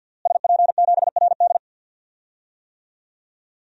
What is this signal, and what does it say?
Morse code, heard as a keyed tone, about 51 wpm, 700 Hz. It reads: SP7RD
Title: DXCC-100 Call Signs - 5 Characters - Repeated 2x - 50wpm